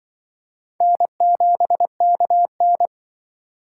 Morse code, heard as a keyed tone, about 25 wpm, 700 Hz. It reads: N7KN
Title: Speed Racing - USA 1x2 Call Signs 20wpm